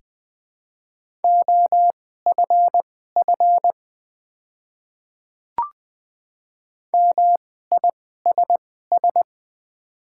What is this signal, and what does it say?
Morse code, heard as a keyed tone, about 20 wpm, spaced as for 15 wpm, 690 Hz. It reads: OFF MISS
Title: 2x Character Spacing - Top 500 Words - 20wpm